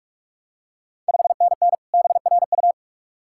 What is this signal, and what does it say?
Morse code, heard as a keyed tone, about 46 wpm, 690 Hz. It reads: 5NN BRU
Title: Ontario QSO Party Exchange 45wpm